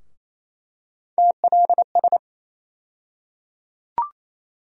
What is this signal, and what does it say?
Morse code, heard as a keyed tone, about 28 wpm, 700 Hz. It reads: TLS
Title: USA Call Sign Postfixes - 3 Letters 28wpm